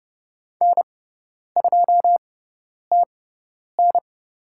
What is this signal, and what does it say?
Morse code, heard as a keyed tone, about 30 wpm, spaced as for 10 wpm, 700 Hz. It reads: N2TN
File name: Lesson 16g - Letter c - Call Signs 30wpm